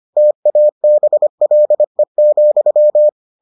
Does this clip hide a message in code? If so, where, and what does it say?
Yes: Morse code from 0.2 to 3.1 s, heard as a keyed tone, about 25 wpm, 600 Hz. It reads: TABLE,